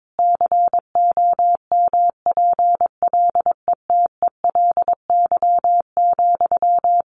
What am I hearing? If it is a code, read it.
Morse code, heard as a keyed tone, about 22 wpm, 700 Hz. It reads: COMPLETELY,
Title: Sentences from Top 1100 Words 22wpm